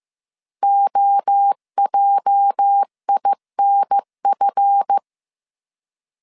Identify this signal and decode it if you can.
Morse code, heard as a keyed tone, about 15 wpm, 800 Hz. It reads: OJINF
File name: CW / DET / 1 9 1500C S A 14wpm 3222